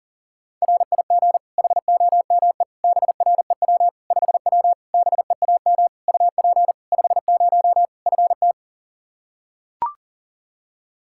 Morse code, heard as a keyed tone, about 37 wpm, 700 Hz. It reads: RIG HOME BREW 5W BEAM UP 50 FT
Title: Practice QSO - Exchange 2 - Info - 40wpm